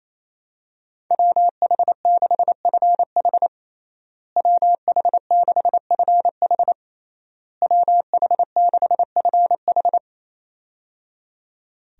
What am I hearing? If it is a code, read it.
Morse code, heard as a keyed tone, about 27 wpm, 700 Hz. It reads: WH6FH WH6FH WH6FH
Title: USA Call Signs - 5 Characters - Repeated 3x - 28wpm